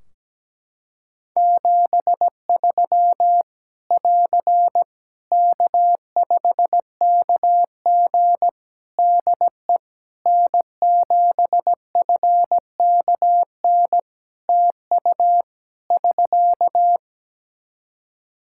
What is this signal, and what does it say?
Morse code, heard as a keyed tone, about 17 wpm, 700 Hz. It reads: 73 <AR> K5KG DE N7FKN TU <SK>
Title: K1USN SST QSO Practice 17wpm